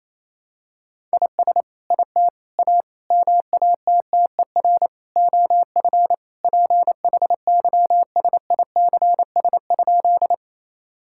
Morse code, heard as a keyed tone, about 28 wpm, 700 Hz. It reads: IS IT A MATTER OF PHYSICS?
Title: Sentences from Top 1000 Words - Review Part 5 - 28wpm